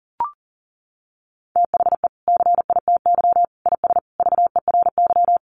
Morse code, heard as a keyed tone, about 40 wpm, 700 Hz. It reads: THE CITY IS VERY
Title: Easy Phrases 40wpm